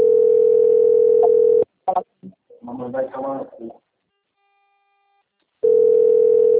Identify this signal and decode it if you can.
Ringback tone (North America), 440 + 480 Hz, with speech.
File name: out- - - - - .19301